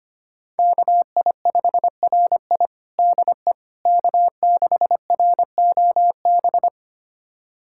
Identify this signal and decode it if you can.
Morse code, heard as a keyed tone, about 25 wpm, 700 Hz. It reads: KI5RI DE K6ROB
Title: USA Callsign DE Callsign - 5 Characters - 25wpm